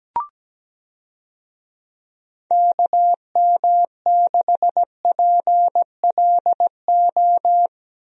Morse code, heard as a keyed tone, about 17 wpm, 700 Hz. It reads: KM6PLO
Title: USA Call Signs - 6 Characters - 17wpm